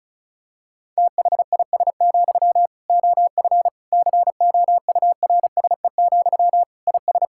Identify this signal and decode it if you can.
Morse code, heard as a keyed tone, about 34 wpm, 700 Hz. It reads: THIS, OF COURSE, IS
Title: Sentences from Top 1600 Words 35wpm